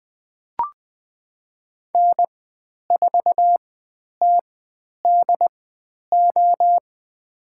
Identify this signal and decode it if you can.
Morse code, heard as a keyed tone, about 20 wpm, spaced as for 10 wpm, 700 Hz. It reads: N4TDO
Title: 102 Lesson 17g - Letter u - Call Signs 20wpm